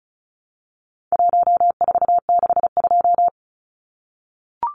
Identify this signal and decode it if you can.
Morse code, heard as a keyed tone, about 34 wpm, 700 Hz. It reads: 1462
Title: Numbers - 4-Digit 35wpm